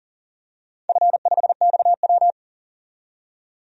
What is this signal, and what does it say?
Morse code, heard as a keyed tone, about 40 wpm, 700 Hz. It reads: F5XW
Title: DXCC-100 Call Signs - 4 Characters - 40wpm